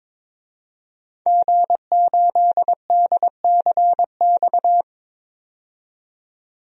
Morse code, heard as a keyed tone, about 22 wpm, 700 Hz. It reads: G8DCX
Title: DXCC-100 Call Signs - 5 Characters - 22wpm